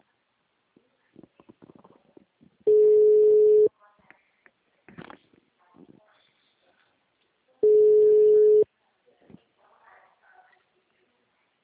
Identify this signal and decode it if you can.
Ringback tone (Europe, 425 Hz), with speech.
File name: out- -2006- - - .68